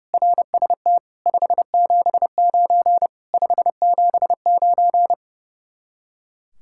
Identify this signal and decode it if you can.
Morse code, heard as a keyed tone, about 36 wpm, 700 Hz. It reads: RST 579 579
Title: Speed-Racing Common QSO Phrases - 30wpm